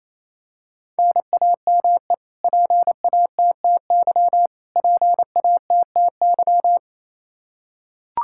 Morse code, heard as a keyed tone, about 28 wpm, 700 Hz. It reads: NAME PATTY PATTY
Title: Practice QSO - Exchange 1 - Beginning - 28wpm